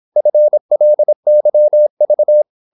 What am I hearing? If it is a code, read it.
Morse code, heard as a keyed tone, about 26 wpm, 600 Hz. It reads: FLYV
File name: Korte ord DK 26wpm